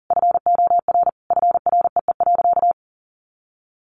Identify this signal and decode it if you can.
Morse code, heard as a keyed tone, about 40 wpm, 700 Hz. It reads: FOR FREE.
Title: Additional Easy Phrases 40wpm